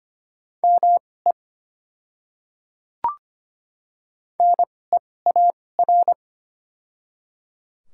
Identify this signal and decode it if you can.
Morse code, heard as a keyed tone, about 25 wpm, spaced as for 18 wpm, 700 Hz. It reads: ME NEAR
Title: 2x Character Spacing - Top 300 Words - 25wpm